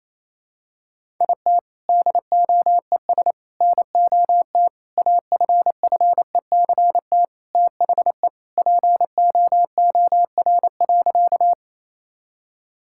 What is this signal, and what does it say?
Morse code, heard as a keyed tone, about 28 wpm, 700 Hz. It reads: IT DOES NOT AFFECT THE POOR.
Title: Sentences from Top 1000 Words - Review Part 6 - 28wpm